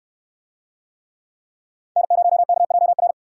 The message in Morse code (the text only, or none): N0KJR